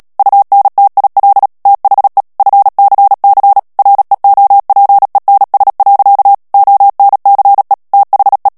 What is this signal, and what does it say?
Morse code, heard as a keyed tone, about 37 wpm, 800 Hz. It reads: UNTIL THE FCC REOPENS. ONCE THE
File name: morse mp3 / ANGLAIS texte clair 3 vitesse 35 mots minute